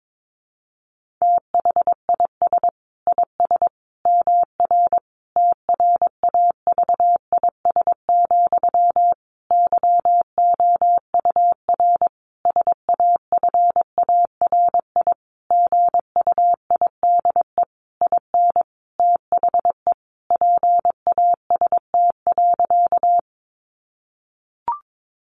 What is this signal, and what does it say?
Morse code, heard as a keyed tone, about 22 wpm, 700 Hz. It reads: THIS IS MR TRAVIS, YOUR SAFARI GUIDE IN THE PAST.
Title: A Sound Of Thunder 22wpm